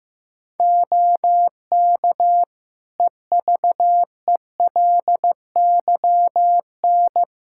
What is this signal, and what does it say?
Morse code, heard as a keyed tone, about 15 wpm, 700 Hz. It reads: OK EVELYN
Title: Practice QSO - Exchange 3 - More Info - 15wpm